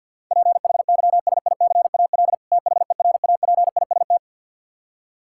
Morse code, heard as a keyed tone, about 48 wpm, 700 Hz. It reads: PHYSICAL THERAPIST
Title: Practice QSO - Exchange 3 - More Info - 50wpm